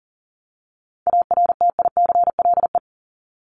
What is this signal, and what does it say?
Morse code, heard as a keyed tone, about 40 wpm, 700 Hz. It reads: ARTICLE
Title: Top 751-1000 Words 40wpm